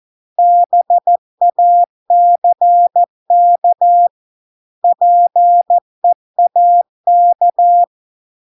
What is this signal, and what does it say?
Morse code, heard as a keyed tone, about 14 wpm, 700 Hz. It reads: BACK PEAK